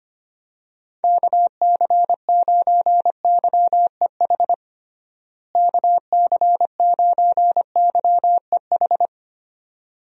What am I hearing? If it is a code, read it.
Morse code, heard as a keyed tone, about 25 wpm, 700 Hz. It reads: KC9YEH KC9YEH
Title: USA Call Signs - 6 Characters - Repeated 2x - 25wpm